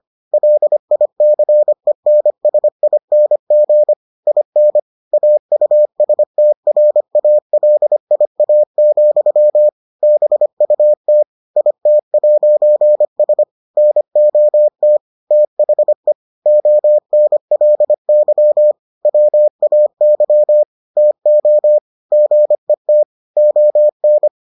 Morse code, heard as a keyed tone, about 25 wpm, 600 Hz. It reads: LICENSING IN AUSTRALIA, BUT IT'S NOT THE ONLY WAY TO GET ON